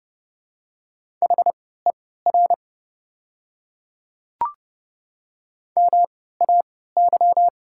Morse code, heard as a keyed tone, about 30 wpm, spaced as for 17 wpm, 690 Hz. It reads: HER MAY
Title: 3x Character Spacing - Top 500 Words - 30wpm